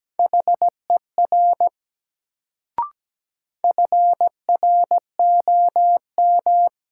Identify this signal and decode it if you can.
Morse code, heard as a keyed tone, about 17 wpm, 700 Hz. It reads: HER FROM
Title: Sentences from Top 100 Words - New Word Drill - 17wpm